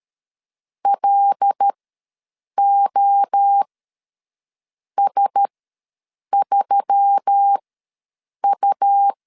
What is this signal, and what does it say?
Morse code, heard as a keyed tone, about 13 wpm, spaced as for 6 wpm, 800 Hz. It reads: LOS3U
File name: LEZIONI / 5 LBVC / 5 LBVC ws 12wpm 3159